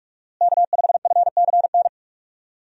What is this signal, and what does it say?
Morse code, heard as a keyed tone, about 46 wpm, 700 Hz. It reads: K5FCN
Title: USA Call Signs - 5 Characters - Repeated 3x - 45wpm